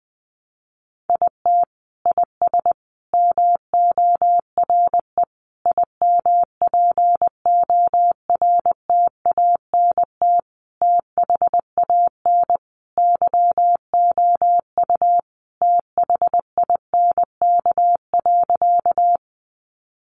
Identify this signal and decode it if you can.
Morse code, heard as a keyed tone, about 20 wpm, 700 Hz. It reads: IT IS MORE IMPORTANT THAN YOU THINK.
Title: Sentences from Top 600 Words 20wpm